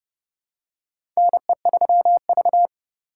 Morse code, heard as a keyed tone, about 30 wpm, 700 Hz. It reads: NE3V
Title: USA Call Signs - 3x Character Spacing - 30wpm